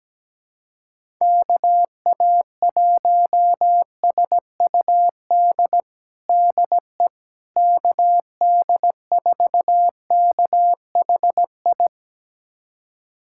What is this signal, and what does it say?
Morse code, heard as a keyed tone, about 17 wpm, 700 Hz. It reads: KA1SUD DE KD4KHI
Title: USA Callsign DE Callsign - 6 Characters - 17wpm